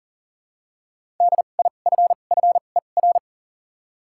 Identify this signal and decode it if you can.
Morse code, heard as a keyed tone, about 40 wpm, 700 Hz. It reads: D I F F E R
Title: 2x Character Spacing - Top 300 Words - 40wpm